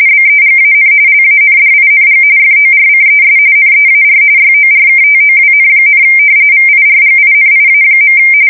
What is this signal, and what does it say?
RTTY, about 45.45 baud. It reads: BEGAN LANDING FORCES NEAR YORKTOWN, VIRGINIA.
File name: fsk170 / usnavy1